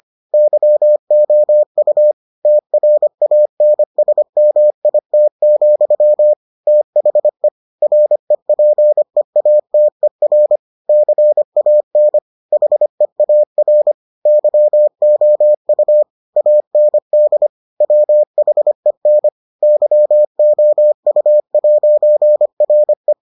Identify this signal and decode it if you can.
Morse code, heard as a keyed tone, about 25 wpm, 600 Hz. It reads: YOU TRANSMIT, THE REPEATER CAN HEAR YOU AND WHEN YOU'RE